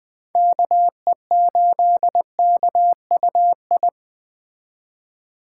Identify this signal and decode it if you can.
Morse code, heard as a keyed tone, about 20 wpm, 700 Hz. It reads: KE8KUI